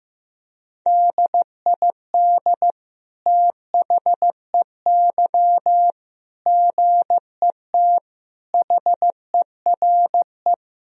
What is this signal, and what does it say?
Morse code, heard as a keyed tone, about 15 wpm, 700 Hz. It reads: DID THEY GET HERE
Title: Sentences from Top 1000 Words - Review Part 3 - 15wpm